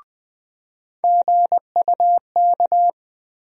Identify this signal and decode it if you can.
Morse code, heard as a keyed tone, about 20 wpm, 700 Hz. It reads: GUK
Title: USA Call Sign Postfixes - 3 Letters 20wpm